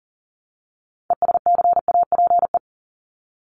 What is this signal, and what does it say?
Morse code, heard as a keyed tone, about 40 wpm, 700 Hz. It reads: ESCAPE
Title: Top 1501-1750 Words 40wpm